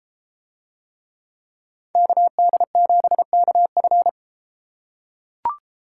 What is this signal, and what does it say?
Morse code, heard as a keyed tone, about 33 wpm, 700 Hz. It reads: KD7KF